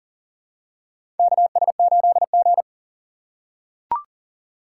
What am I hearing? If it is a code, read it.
Morse code, heard as a keyed tone, about 40 wpm, 700 Hz. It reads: KS8G
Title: USA Call Signs - 4 Characters - 3x Character Spacing - 40wpm